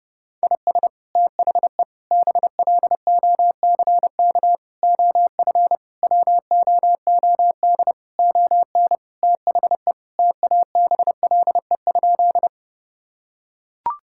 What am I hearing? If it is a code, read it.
Morse code, heard as a keyed tone, about 30 wpm, 700 Hz. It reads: IS THE BLOCK OF WOOD ON THE TABLE?
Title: Sentences from Top 1600 Words 30wpm